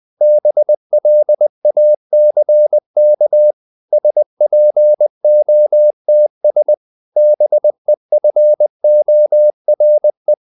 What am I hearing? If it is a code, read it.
Morse code, heard as a keyed tone, about 20 wpm, 600 Hz. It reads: BLACK SPOTS BEFORE
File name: the old man and the sea part 2 20wpm